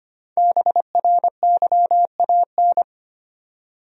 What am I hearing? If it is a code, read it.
Morse code, heard as a keyed tone, about 25 wpm, 700 Hz. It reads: BRYAN
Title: USA Common Names - Max 5 Letters - 25wpm